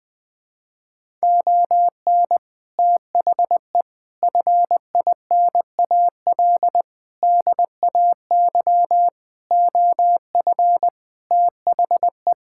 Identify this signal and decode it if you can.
Morse code, heard as a keyed tone, about 21 wpm, 700 Hz. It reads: ON THE FINAL DAY OF THE
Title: Sentences with 7-Word Phrases - Spoken Beginning - 20wpm